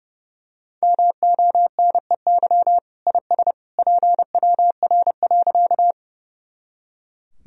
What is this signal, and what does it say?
Morse code, heard as a keyed tone, about 30 wpm, 700 Hz. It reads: MONEY IS PWR.
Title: Sentences from Top 500 Words and Abbreviations 30wpm